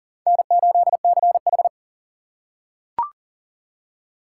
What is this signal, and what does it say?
Morse code, heard as a keyed tone, about 40 wpm, 700 Hz. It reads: N8CH
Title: USA Call Signs - 4 Characters - 2x Character Spacing - 40wpm